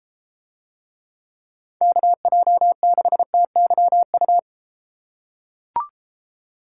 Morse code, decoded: KJ6TYU

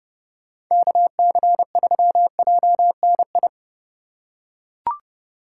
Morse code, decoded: KC3JNI

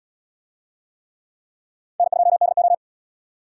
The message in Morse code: N2DO

